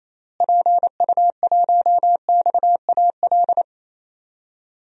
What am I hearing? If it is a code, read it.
Morse code, heard as a keyed tone, about 28 wpm, 700 Hz. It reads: PU1XAL
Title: DXCC-100 Call Signs - 6 Characters - Repeated 3x - 28wpm